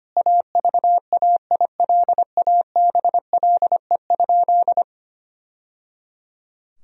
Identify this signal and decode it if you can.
Morse code, heard as a keyed tone, about 25 wpm, 700 Hz. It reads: AVAILABLE?
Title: Sentences from Top 1000 Words - Review Part 3 - 25wpm